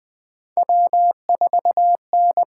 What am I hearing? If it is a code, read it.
Morse code, heard as a keyed tone, about 20 wpm, 700 Hz. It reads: W4N
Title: USA Call Signs - 8x Character Spacing - 20wpm